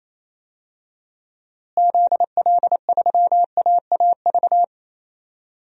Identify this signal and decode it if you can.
Morse code, heard as a keyed tone, about 27 wpm, 700 Hz. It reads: ZL3AAV
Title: DXCC-100 Call Signs - 6 Characters - Repeated 2x - 28wpm